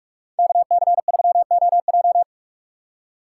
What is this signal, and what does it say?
Morse code, heard as a keyed tone, about 55 wpm, 700 Hz. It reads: KC3OJ
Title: Speed Racing - USA 2x2 Call Signs 45wpm